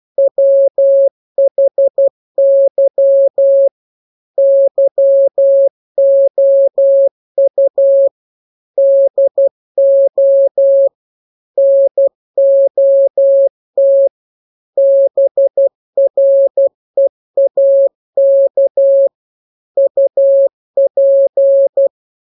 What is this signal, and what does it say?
Morse code, heard as a keyed tone, about 12 wpm, 550 Hz. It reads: WHY YOU DO NOT BREAK UP
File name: jokes / joke 18 12 0000